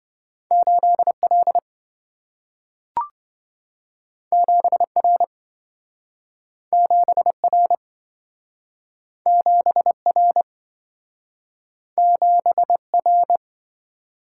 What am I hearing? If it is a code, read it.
Morse code, heard as a keyed tone, about 27 wpm, 700 Hz. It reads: 8L 7R 7R 7R 7R